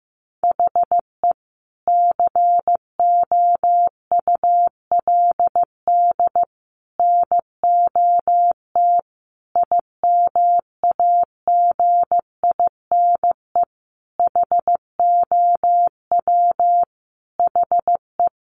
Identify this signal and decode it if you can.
Morse code, heard as a keyed tone, about 15 wpm, 700 Hz. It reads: HE COULD NOT IMAGINE HOW HE
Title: Sentences from Top 1400 Words 15wpm